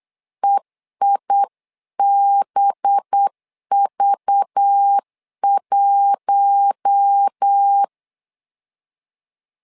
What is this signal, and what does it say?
Morse code, heard as a keyed tone, about 8 wpm, 800 Hz. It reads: EIBV1
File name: LEZIONI / 5 LBVC / 5 LBV 8wpm 2837